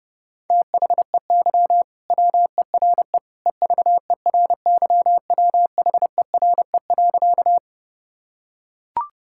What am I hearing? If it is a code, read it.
Morse code, heard as a keyed tone, about 30 wpm, 700 Hz. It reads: THEY WERE EVERYWHERE.
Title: Sentences from Top 1600 Words 30wpm